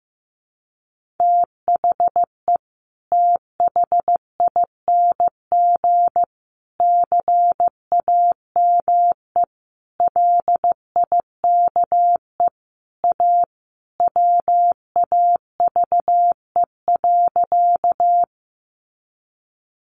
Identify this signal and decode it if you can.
Morse code, heard as a keyed tone, about 15 wpm, 700 Hz. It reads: THE THING CAME LIKE A WAVE.
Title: Sentences from Top 1400 Words 15wpm